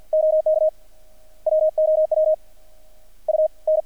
Morse code, heard as a keyed tone, about 50 wpm, 610 Hz. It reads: OK WOW U N
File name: archived / CRSnet